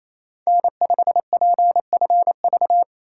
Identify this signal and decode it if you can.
Morse code, heard as a keyed tone, about 28 wpm, 700 Hz. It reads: N5PFV
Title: USA Call Signs - 2x Character Spacing - 28wpm